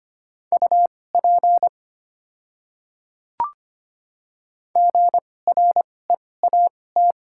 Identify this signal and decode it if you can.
Morse code, heard as a keyed tone, about 25 wpm, spaced as for 18 wpm, 700 Hz. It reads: UP GREAT
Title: 2x Character Spacing - Top 200 Words - 25wpm